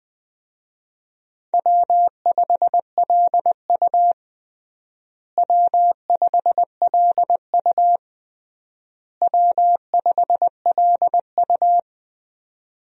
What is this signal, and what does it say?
Morse code, heard as a keyed tone, about 20 wpm, 700 Hz. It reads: W5LU W5LU W5LU
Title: USA Call Signs - 4 Characters - Repeated 3x - 20wpm